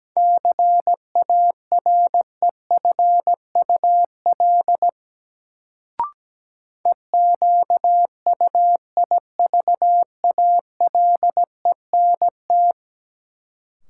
Morse code, heard as a keyed tone, about 17 wpm, 700 Hz. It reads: CAREFUL EQUIVALENT
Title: Sentences from Top 1600 Words - New Word Drill Alt - 17wpm